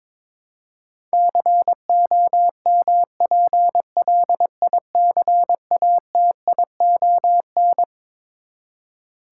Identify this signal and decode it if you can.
Morse code, heard as a keyed tone, about 21 wpm, 700 Hz. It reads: COMPLICATION
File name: practice / 22wpm / Twelve Letter Words - 22wpm